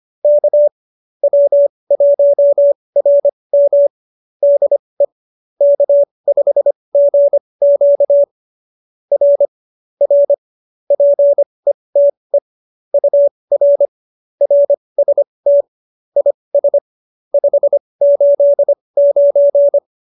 Morse code, heard as a keyed tone, about 25 wpm, spaced as for 20 wpm, 570 Hz. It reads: K W1RM DE K5GQ R R PETE UR RST IS 589